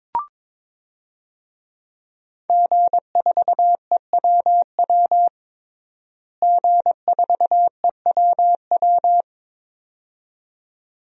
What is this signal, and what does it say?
Morse code, heard as a keyed tone, about 22 wpm, 700 Hz. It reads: G4EWW G4EWW
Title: DXCC-100 Call Signs - 5 Characters - Repeated 2x - 22wpm